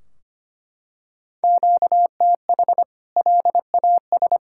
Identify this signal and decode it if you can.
Morse code, heard as a keyed tone, about 25 wpm, 700 Hz. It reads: QTH LAS
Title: K1USN SST QSO Practice 25wpm